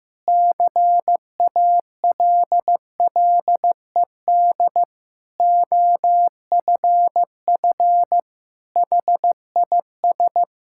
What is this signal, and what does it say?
Morse code, heard as a keyed tone, about 14 wpm, 700 Hz. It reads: CALLED OFF HIS
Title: Pangram Challenge v2 - 15wpm